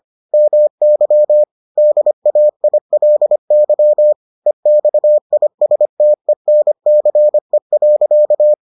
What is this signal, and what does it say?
Morse code, heard as a keyed tone, about 25 wpm, 600 Hz. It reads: MY DAILY EXISTENCE.